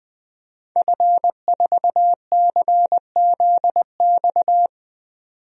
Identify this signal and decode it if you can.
Morse code, heard as a keyed tone, about 20 wpm, 700 Hz. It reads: F4CZX
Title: DXCC-100 Call Signs - 5 Characters - 20wpm